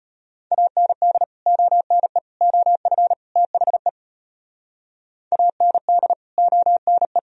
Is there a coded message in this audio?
Yes: Morse code from 0.5 to 7.2 s, heard as a keyed tone, about 32 wpm, 700 Hz. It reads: AND ONE OF THE AND ONE